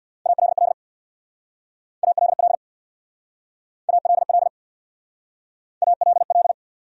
Morse code, heard as a keyed tone, about 67 wpm, 700 Hz. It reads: ALL ALL ALL ALL